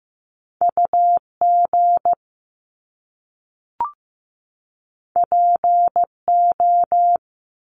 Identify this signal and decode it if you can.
Morse code, heard as a keyed tone, about 15 wpm, 700 Hz. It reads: UG PO
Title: Top Two Letter Combinations in Words 15wpm